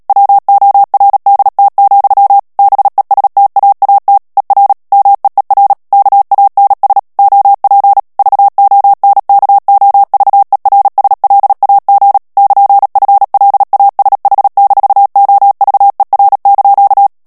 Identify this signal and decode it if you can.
Morse code, heard as a keyed tone, about 37 wpm, 800 Hz. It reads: WORDT, BESTAAT ER MEER KANS OP VONKOVERSLAG <KN>FLASH-OVER)
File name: morse mp3 / HOLLANDAIS texte clair 2 vitesse 35 mots minute